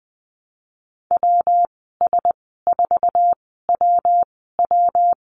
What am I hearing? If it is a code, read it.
Morse code, heard as a keyed tone, about 20 wpm, spaced as for 15 wpm, 700 Hz. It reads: WS4WW
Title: 122 Lesson 20g - Number 3 - Call Signs 20wpm